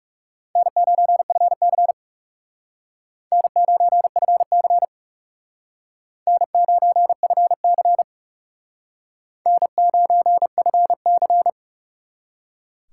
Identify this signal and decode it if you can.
Morse code, heard as a keyed tone, about 40 wpm, 700 Hz. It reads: N9FC N9FC N9FC N9FC